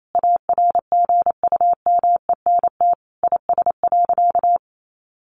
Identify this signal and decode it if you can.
Morse code, heard as a keyed tone, about 28 wpm, 700 Hz. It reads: ARGUMENT IS.
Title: Sentences from Top 1400 Words 28wpm